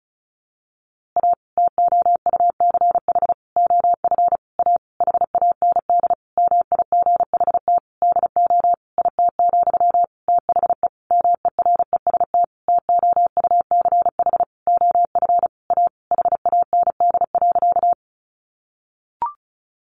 Morse code, heard as a keyed tone, about 34 wpm, 700 Hz. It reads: A TOUCH OF A HAND MIGHT DO IT, THE MEREST TOUCH OF A HAND.
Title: A Sound Of Thunder 35wpm